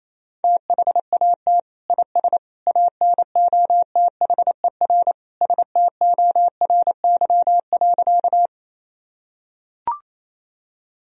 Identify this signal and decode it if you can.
Morse code, heard as a keyed tone, about 27 wpm, 700 Hz. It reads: THAT IS ANOTHER STORY.
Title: Sentences from Top 1000 Words - Review Part 1 - 28wpm